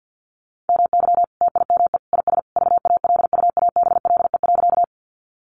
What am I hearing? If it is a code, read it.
Morse code, heard as a keyed tone, about 50 wpm, 700 Hz. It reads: MY TIME IS VALUABLE.
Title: Sentences from Top 1100 Words 50wpm